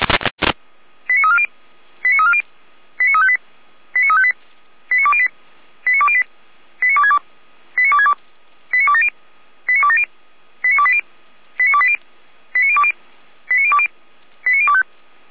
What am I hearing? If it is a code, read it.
Five-tone selective call (ZVEI), heard as quick runs of five tones, tones of about 70 ms. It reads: 89340 89340 89358 89358 89208 89208 89262 89262 89270 89270 89280 89280 89020 89020 89025